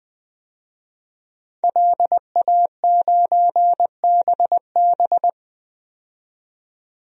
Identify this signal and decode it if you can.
Morse code, heard as a keyed tone, about 20 wpm, 700 Hz. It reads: LA9BB
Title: DXCC-100 Call Signs - 5 Characters - 20wpm